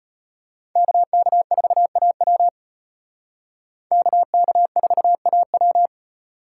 Morse code, heard as a keyed tone, about 32 wpm, 700 Hz. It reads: KK4AW KK4AW